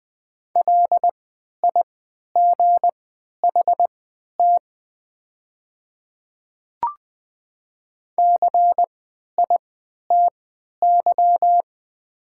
Morse code, heard as a keyed tone, about 20 wpm, spaced as for 11 wpm, 700 Hz. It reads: LIGHT CITY
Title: 3x Character Spacing - Top 300 Words - 20wpm